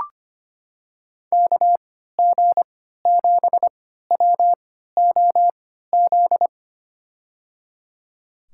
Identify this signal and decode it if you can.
Morse code, heard as a keyed tone, about 25 wpm, spaced as for 14 wpm, 700 Hz. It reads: KG7WOZ
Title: USA Call Signs - 6 Characters - 3x Character Spacing - 25wpm